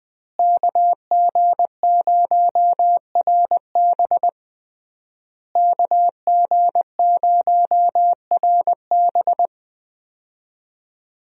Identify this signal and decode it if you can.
Morse code, heard as a keyed tone, about 20 wpm, 700 Hz. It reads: KG0RB KG0RB
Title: USA Call Signs - 5 Characters - Repeated 3x - 20wpm